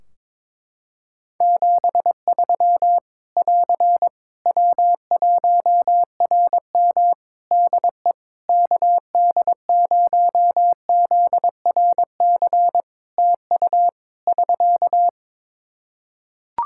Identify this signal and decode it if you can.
Morse code, heard as a keyed tone, about 22 wpm, 700 Hz. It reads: 73 <AR> W1RM DE KD0ZRC TU <SK>
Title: Practice QSO - Exchange 4 - Ending - 22wpm